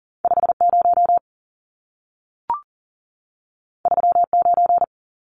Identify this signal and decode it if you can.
Morse code, heard as a keyed tone, about 40 wpm, 700 Hz. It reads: 50 39
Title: Numbers - 2-Digit 40wpm